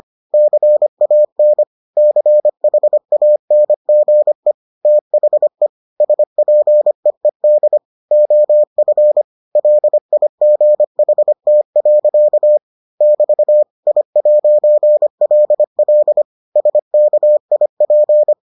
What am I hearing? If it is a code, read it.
Morse code, heard as a keyed tone, about 25 wpm, 600 Hz. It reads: CAN CHANGE THE SPEED OF LIGHT. <BT> I'LL SKIP